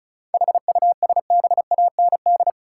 Morse code, heard as a keyed tone, about 34 wpm, 700 Hz. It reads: HUSBAND
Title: Seven Letter Words - 35wpm